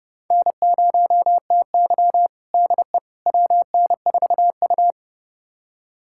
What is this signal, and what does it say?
Morse code, heard as a keyed tone, about 32 wpm, 700 Hz. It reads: N0TY DE WN4U
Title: USA Callsign DE Callsign - 4 Characters - 30wpm